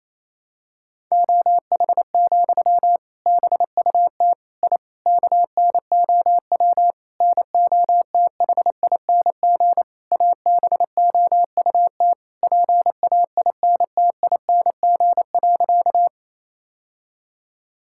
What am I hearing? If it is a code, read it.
Morse code, heard as a keyed tone, about 28 wpm, 700 Hz. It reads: OH, BUT I KNOW NOTHING ABOUT PAINTING.
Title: Sentences from Top 1000 Words - Review Part 6 - 28wpm